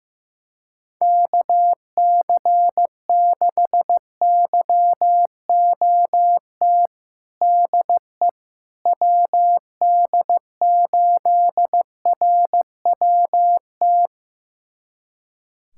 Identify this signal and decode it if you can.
Morse code, heard as a keyed tone, about 15 wpm, 700 Hz. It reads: KC6YOT DE WD8RWT
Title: USA Callsign DE Callsign 15wpm